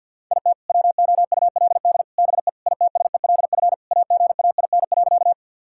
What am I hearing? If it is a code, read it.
Morse code, heard as a keyed tone, about 51 wpm, 700 Hz. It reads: IT WOULD BE ITSELF AGAIN.
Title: Sentences from Top 1000 Words - Review Part 5 - 50wpm